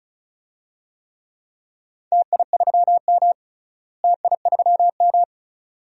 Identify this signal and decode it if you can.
Morse code, heard as a keyed tone, about 36 wpm, 700 Hz. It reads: TI3M TI3M
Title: DXCC-100 Call Signs - 4 Characters - Repeated 2x - 35wpm